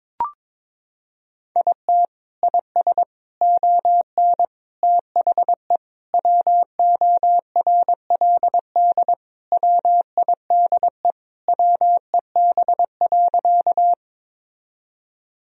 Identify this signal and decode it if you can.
Morse code, heard as a keyed tone, about 22 wpm, 700 Hz. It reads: IT IS ON THE WORLD WIDE WEB.
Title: Sentences from Top 1000 Words - Review Part 4 - 22wpm